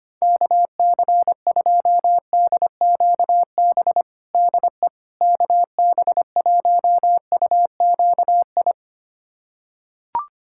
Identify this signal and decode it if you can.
Morse code, heard as a keyed tone, about 25 wpm, 700 Hz. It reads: KC2DQB DE KB1UQI
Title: USA Callsign DE Callsign 25wpm